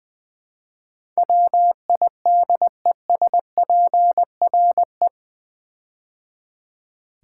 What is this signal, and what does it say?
Morse code, heard as a keyed tone, about 20 wpm, 700 Hz. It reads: WIDESPRE
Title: Build A Word - 4 Letters At A Time - 10-Letter Words - 20wpm